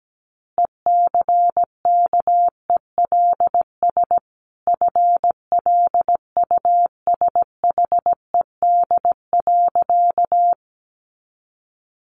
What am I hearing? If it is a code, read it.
Morse code, heard as a keyed tone, about 17 wpm, 700 Hz. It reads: ECKELS FLUSHED.
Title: A Sound Of Thunder 17wpm